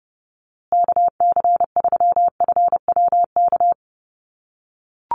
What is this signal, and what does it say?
Morse code, heard as a keyed tone, about 30 wpm, 700 Hz. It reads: KC3FWK
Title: USA Call Signs Repeated 2x 30wpm